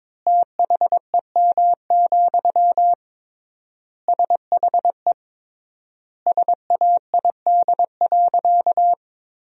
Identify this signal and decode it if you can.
Morse code, heard as a keyed tone, about 22 wpm, 700 Hz. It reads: THEM, SHE SAID.
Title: 3x Word-Spacing - Sentences from Top 400 Words - 22wpm